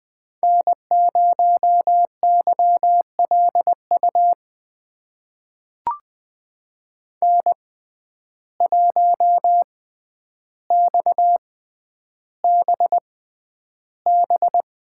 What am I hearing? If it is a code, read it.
Morse code, heard as a keyed tone, about 20 wpm, spaced as for 10 wpm, 700 Hz. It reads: N0YLU N1XBB
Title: USA Call Signs - 5 Characters - 6x Character Spacing - 20wpm